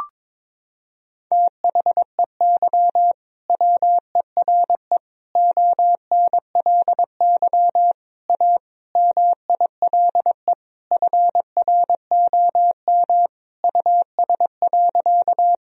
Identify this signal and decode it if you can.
Morse code, heard as a keyed tone, about 22 wpm, 700 Hz. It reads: THEY WERE ONLY A MILE FROM US.
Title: Sentences from Top 1000 Words - Review Part 1 - 22wpm